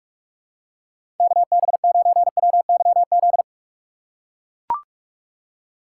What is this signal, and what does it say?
Morse code, heard as a keyed tone, about 44 wpm, 700 Hz. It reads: KB9WYZ
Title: USA Call Signs - 6 Characters - 6x Character Spacing - 45wpm